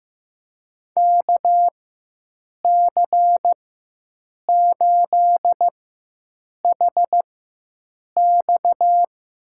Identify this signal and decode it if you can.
Morse code, heard as a keyed tone, about 15 wpm, spaced as for 7 wpm, 700 Hz. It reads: KC8HX
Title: USA Call Signs - 4x Character Spacing - 15wpm